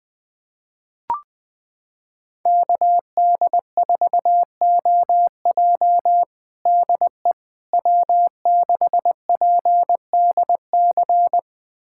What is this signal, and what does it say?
Morse code, heard as a keyed tone, about 20 wpm, 700 Hz. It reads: KD4OJ DE W6PDC
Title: USA Callsign DE Callsign - 5 Characters - 20wpm